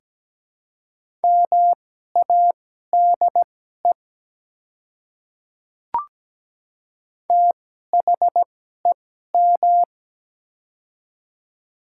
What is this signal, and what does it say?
Morse code, heard as a keyed tone, about 17 wpm, spaced as for 12 wpm, 700 Hz. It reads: MADE THEM
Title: 2x Character Spacing - Top 200 Words - 17wpm